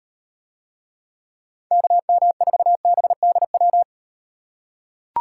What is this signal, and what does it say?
Morse code, heard as a keyed tone, about 38 wpm, 700 Hz. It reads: KM4BDW